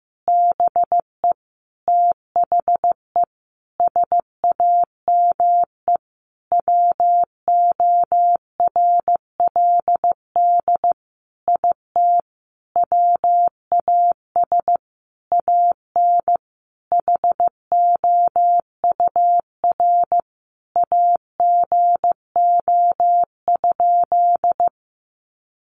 Morse code, heard as a keyed tone, about 15 wpm, 700 Hz. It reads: BE THE SAME WORLD IT WAS AN HOUR AGO?
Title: Sentences from Top 1300 Words 15wpm